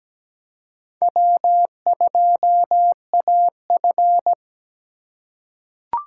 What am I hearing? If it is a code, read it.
Morse code, heard as a keyed tone, about 17 wpm, 700 Hz. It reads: W2AF
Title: USA Call Signs - 4 Characters - 3x Character Spacing - 17wpm